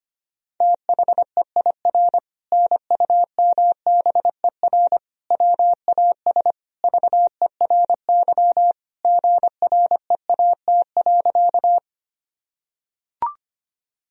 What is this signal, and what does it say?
Morse code, heard as a keyed tone, about 25 wpm, 700 Hz. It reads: THEIR NUMBER WAS VERY GREAT.
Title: Sentences from Top 1000 Words - Review Part 1 - 25wpm